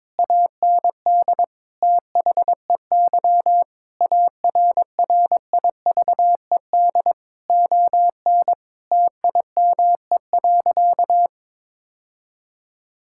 Morse code, heard as a keyed tone, about 22 wpm, 700 Hz. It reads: AND THEY ARRIVED ON TIME.
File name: practice / 22wpm / Sentences with 2-Word Phrases - 22wpm